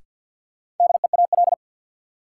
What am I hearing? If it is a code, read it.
Morse code, heard as a keyed tone, about 51 wpm, 700 Hz. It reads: DEAL
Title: Sentences from Top 800 Words - New Word Drill - 50wpm